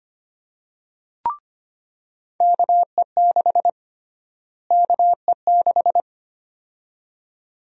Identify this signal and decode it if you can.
Morse code, heard as a keyed tone, about 25 wpm, 700 Hz. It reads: KE6 KE6
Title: USA Call Sign Prefixes - Repeated 2x - 25wpm